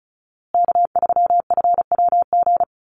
Morse code, heard as a keyed tone, about 34 wpm, 700 Hz. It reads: K3FWG
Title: USA Call Signs Repeated 3x 35wpm